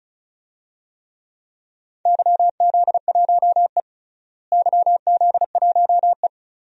Morse code, heard as a keyed tone, about 36 wpm, 700 Hz. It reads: YZ1E YZ1E
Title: DXCC-100 Call Signs - 4 Characters - Repeated 2x - 35wpm